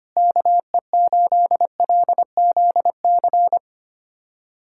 Morse code, heard as a keyed tone, about 25 wpm, 700 Hz. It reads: KE8LZC
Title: USA Callsign DE Callsign - 6 Characters - 25wpm